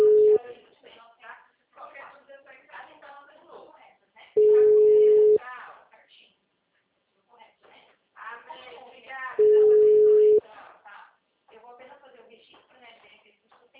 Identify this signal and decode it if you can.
Ringback tone (Europe, 425 Hz), with speech.